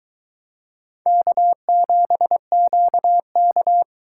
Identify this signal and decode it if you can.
Morse code, heard as a keyed tone, about 22 wpm, 700 Hz. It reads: K7QK